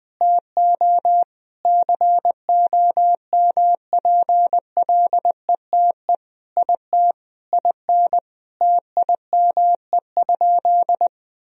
Morse code, heard as a keyed tone, about 20 wpm, 700 Hz. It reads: TO COMPLETE IT IN TIME?
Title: Sentences from Top 1000 Words - Review Part 3 - 20wpm